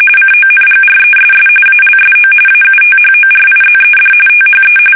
RTTY, about 45.45 baud. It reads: CONGRESS ALSO AUTHORIZED THE
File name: fsk850 / usnavy3